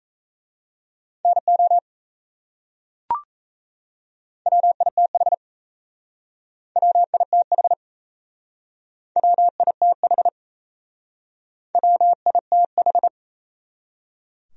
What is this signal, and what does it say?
Morse code, heard as a keyed tone, about 38 wpm, 700 Hz. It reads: NO WITH WITH WITH WITH